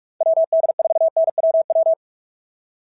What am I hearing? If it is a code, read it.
Morse code, heard as a keyed tone, about 46 wpm, 680 Hz. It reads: WD4NWW
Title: SOTA QSO Practice 45wpm